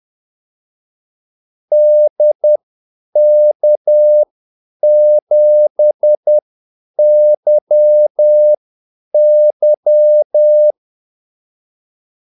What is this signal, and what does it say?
Morse code, heard as a keyed tone, about 10 wpm, spaced as for 8 wpm, 600 Hz. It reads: DK7YY
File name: Callsigns-10-8wpm